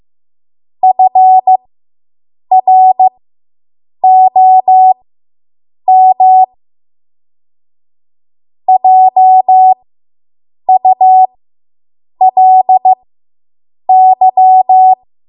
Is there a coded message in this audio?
Yes: Morse code from 0.8 to 14.9 s, heard as a keyed tone, about 15 wpm, spaced as for 7 wpm, 750 Hz. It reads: FROM JULY